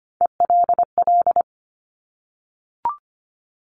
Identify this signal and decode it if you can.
Morse code, heard as a keyed tone, about 25 wpm, 700 Hz. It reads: ELL
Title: Top Three Letter Combinations in Words 25wpm